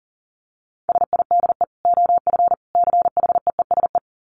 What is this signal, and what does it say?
Morse code, heard as a keyed tone, about 37 wpm, 700 Hz. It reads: SIDE OF CHEESE
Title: Dit Challenge 40wpm